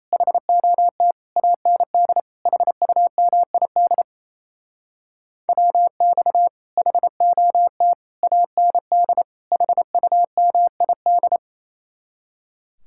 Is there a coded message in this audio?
Yes: Morse code from 0.1 to 11.4 s, heard as a keyed tone, about 37 wpm, 700 Hz. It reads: HOT AND HUMID WX HOT AND HUMID